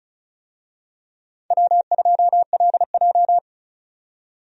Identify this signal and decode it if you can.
Morse code, heard as a keyed tone, about 34 wpm, 700 Hz. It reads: W2LJ
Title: USA Call Signs - 4 Characters - 35wpm